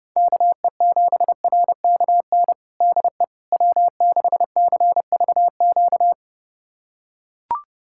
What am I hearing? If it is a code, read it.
Morse code, heard as a keyed tone, about 30 wpm, 700 Hz. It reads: KE7RKN DE W6CVQ
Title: USA Callsign DE Callsign 30wpm